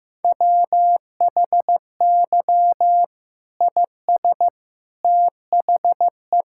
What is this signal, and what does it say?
Morse code, heard as a keyed tone, about 16 wpm, 700 Hz. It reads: WHY IS THE
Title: Sentences from Top 1600 Words 15wpm